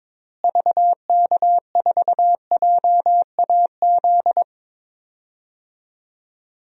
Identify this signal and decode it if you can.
Morse code, heard as a keyed tone, about 21 wpm, 700 Hz. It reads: VK4JAZ
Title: DXCC-100 Call Signs - 6 Characters - Repeated 3x - 22wpm